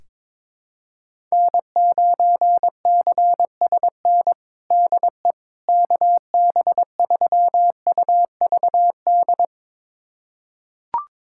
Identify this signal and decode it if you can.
Morse code, heard as a keyed tone, about 21 wpm, 700 Hz. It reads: N9CSN DE KB3UVD
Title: USA Callsign DE Callsign 22wpm